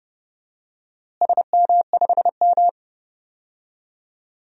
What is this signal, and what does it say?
Morse code, heard as a keyed tone, about 30 wpm, 700 Hz. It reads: SM5M
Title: DXCC-100 Call Signs - 4 Characters - Repeated 3x - 30wpm